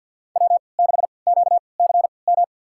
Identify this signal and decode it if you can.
Morse code, heard as a keyed tone, about 48 wpm, spaced as for 28 wpm, 700 Hz. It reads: W6QCM
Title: USA Call Signs - 5 Characters - 3x Character Spacing - 50wpm